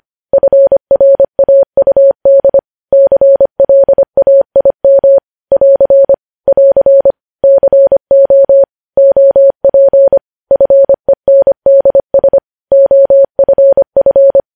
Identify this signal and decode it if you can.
Morse code, heard as a keyed tone, about 25 wpm, 550 Hz. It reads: FRAUD CLAIM <AR> <AR> CO OP FENDS OFF
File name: podcast / News headlines-1x-25x25